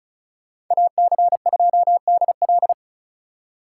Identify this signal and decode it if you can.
Morse code, heard as a keyed tone, about 43 wpm, 700 Hz. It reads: AC2DL
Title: Speed Racing - USA Call Signs 35wpm